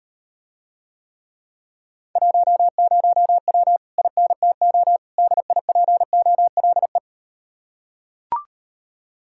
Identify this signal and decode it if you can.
Morse code, heard as a keyed tone, about 38 wpm, 700 Hz. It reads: 10W INTO DIPOLE